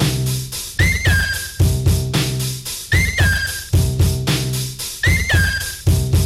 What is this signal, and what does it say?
Emergency Alert System (SAME) tones (header bursts), heard as harsh buzzing bursts. It is heard from 0.8 to 5.8 s.